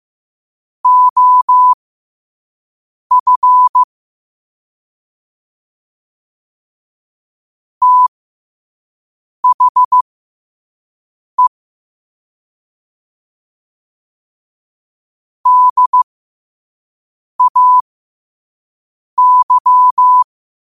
Morse code, heard as a keyed tone, about 15 wpm, spaced as for 5 wpm, 1000 Hz. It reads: OF THE DAY
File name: qotdfiles / 5wpm